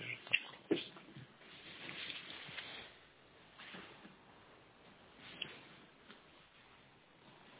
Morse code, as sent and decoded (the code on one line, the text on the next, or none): none